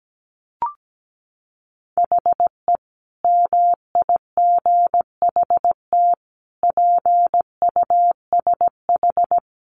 Morse code, heard as a keyed tone, about 17 wpm, 700 Hz. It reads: HE MIGHT PUSH
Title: Sentences from Top 1300 Words 17wpm